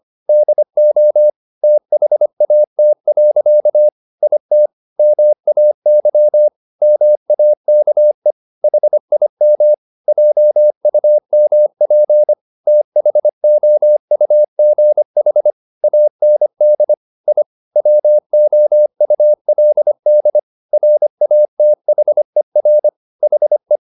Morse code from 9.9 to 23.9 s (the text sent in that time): JUMP THOUGH AND I WOULD RATHER HE